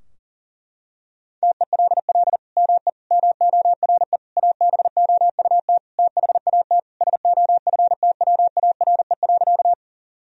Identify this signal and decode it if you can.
Morse code, heard as a keyed tone, about 40 wpm, 700 Hz. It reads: TELL ME MORE ABOUT THAT SOFTWARE.
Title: Sentences from Top 1000 Words - Review Part 3 - 40wpm